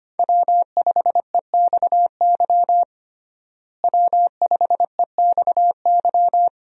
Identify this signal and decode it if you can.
Morse code, heard as a keyed tone, about 25 wpm, 700 Hz. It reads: W5EXY W5EXY
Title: USA Call Signs - 5 Characters - Repeated 2x - 25wpm